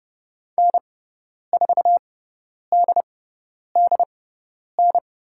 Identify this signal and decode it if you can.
Morse code, heard as a keyed tone, about 30 wpm, spaced as for 10 wpm, 690 Hz. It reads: N4DDN
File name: Lesson 17g - Letter u - Call Signs 30wpm